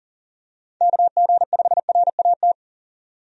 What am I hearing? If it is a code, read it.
Morse code, heard as a keyed tone, about 37 wpm, 700 Hz. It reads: KG5RAT
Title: USA Call Signs - 6 Characters - 40wpm